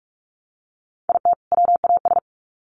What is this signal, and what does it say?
Morse code, heard as a keyed tone, about 44 wpm, 700 Hz. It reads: IT WAS
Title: Sets of 2 Words 45wpm